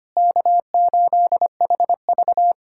Morse code, heard as a keyed tone, about 25 wpm, 690 Hz. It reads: K8HV
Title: SOTA QSO Practice 25wpm